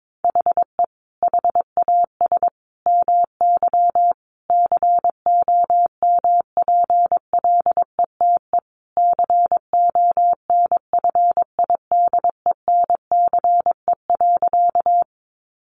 Morse code, heard as a keyed tone, about 22 wpm, 700 Hz. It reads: HE HAS MY COMPLETE CONFIDENCE.
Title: Sentences from Top 1500 Words 22wpm